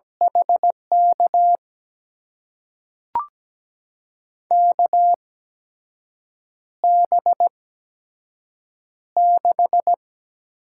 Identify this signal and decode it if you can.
Morse code, heard as a keyed tone, about 17 wpm, spaced as for 6 wpm, 700 Hz. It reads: HK KB6
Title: USA Call Signs - 6 Characters - 8x Character Spacing - 17wpm